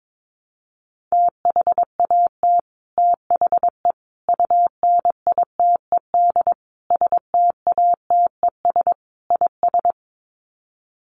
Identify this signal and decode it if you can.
Morse code, heard as a keyed tone, about 22 wpm, 700 Hz. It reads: THAT THE UNITED STATES IS
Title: Sets of 5 Words Encore 22wpm